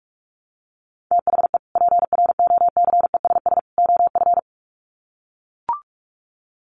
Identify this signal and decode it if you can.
Morse code, heard as a keyed tone, about 44 wpm, 700 Hz. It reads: THE PROCESS OF
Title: Sets of 3 Words 45wpm